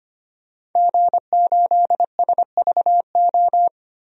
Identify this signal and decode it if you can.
Morse code, heard as a keyed tone, about 25 wpm, 700 Hz. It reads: G8SVO
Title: DXCC-100 Call Signs - 5 Characters - 25wpm